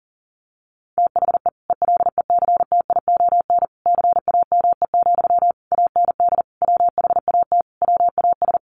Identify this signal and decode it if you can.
Morse code, heard as a keyed tone, about 40 wpm, 700 Hz. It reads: THE ELECTION CAME, AND WHAT WAS
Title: Sentences from Top 1500 Words 40wpm